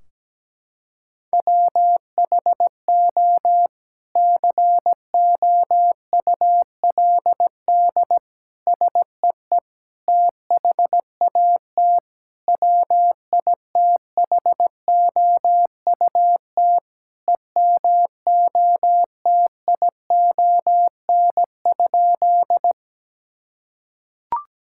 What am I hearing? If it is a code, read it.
Morse code, heard as a keyed tone, about 17 wpm, 700 Hz. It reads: WHO COULD SEE THAT WITHOUT EMOTION?
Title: Sentences from Top 1600 Words 17wpm